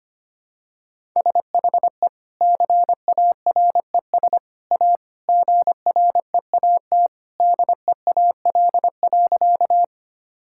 Morse code, heard as a keyed tone, about 25 wpm, 700 Hz. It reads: SHE CARES A GREAT DEAL.
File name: practice / 25wpm / Sentences with 3-Word Phrases - 25wpm